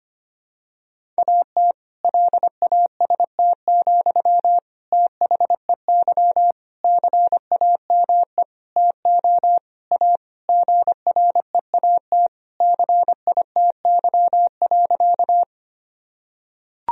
Morse code, heard as a keyed tone, about 25 wpm, 700 Hz. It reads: AT LAST, THEY CAME TO A GREAT CITY.
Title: Sentences from Top 1000 Words - Review Part 1 - 25wpm